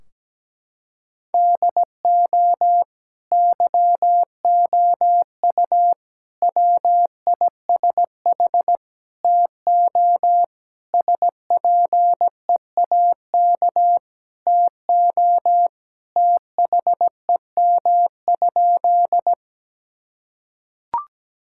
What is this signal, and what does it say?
Morse code, heard as a keyed tone, about 17 wpm, 700 Hz. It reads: DO YOU WISH TO SPEAK TO THEM?
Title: Sentences from Top 1600 Words 17wpm